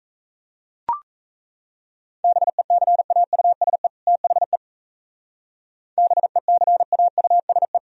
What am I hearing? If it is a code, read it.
Morse code, heard as a keyed tone, about 35 wpm, 700 Hz. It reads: BECAUSE THE BECAUSE